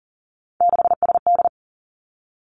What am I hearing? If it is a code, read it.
Morse code, heard as a keyed tone, about 38 wpm, 690 Hz. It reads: 6SD
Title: Three Challenging Letter-Number Combinations 40wpm